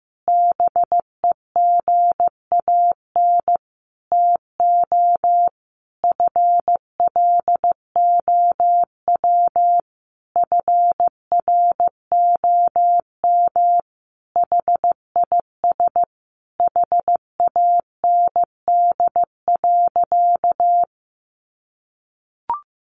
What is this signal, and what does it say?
Morse code, heard as a keyed tone, about 15 wpm, 700 Hz. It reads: BEGAN TO FLOW FROM HIS HAND.
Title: Sentences from Top 1200 Words 15wpm